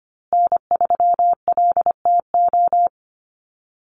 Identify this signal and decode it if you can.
Morse code, heard as a keyed tone, about 25 wpm, 700 Hz. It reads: N3LTO
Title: USA Call Signs Repeated 3x 25wpm